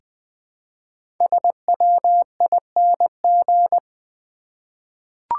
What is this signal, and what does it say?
Morse code, heard as a keyed tone, about 20 wpm, 700 Hz. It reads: SWING
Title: Easy Words with Hints - 20wpm